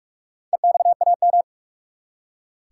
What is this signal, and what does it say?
Morse code, heard as a keyed tone, about 46 wpm, 700 Hz. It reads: EXAM
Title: Four Letter Words - 45wpm